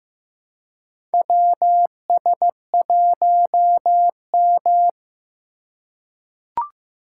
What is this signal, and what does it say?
Morse code, heard as a keyed tone, about 15 wpm, 700 Hz. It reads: WS1M
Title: USA Call Signs - 4 Characters - 2x Character Spacing - 15wpm